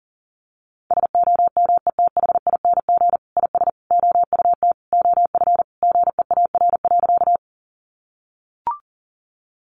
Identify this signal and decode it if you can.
Morse code, heard as a keyed tone, about 37 wpm, 700 Hz. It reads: SOMETHING IS OUT OF GEAR.
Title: Sentences from Top 1400 Words 40wpm